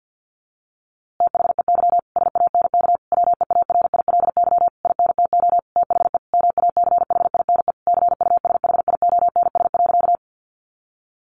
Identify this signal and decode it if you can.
Morse code, heard as a keyed tone, about 50 wpm, 700 Hz. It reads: THEY SANK WEARILY INTO THE MACHINE CUSHIONS.
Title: A Sound Of Thunder 50wpm